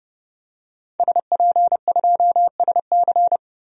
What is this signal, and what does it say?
Morse code, heard as a keyed tone, about 30 wpm, 700 Hz. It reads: SP2SC
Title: DXCC-100 Call Signs - 5 Characters - Repeated 3x - 30wpm